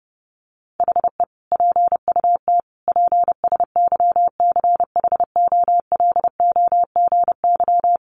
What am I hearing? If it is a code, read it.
Morse code, heard as a keyed tone, about 30 wpm, 700 Hz. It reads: HE PUT PSYCHOLOGY
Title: Sentences from Top 1400 Words 30wpm